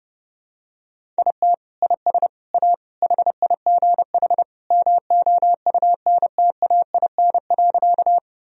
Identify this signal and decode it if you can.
Morse code, heard as a keyed tone, about 30 wpm, 700 Hz. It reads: IT IS A HIGH MOUNTAIN.
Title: Sentences from Top 1000 Words - Review Part 1 - 30wpm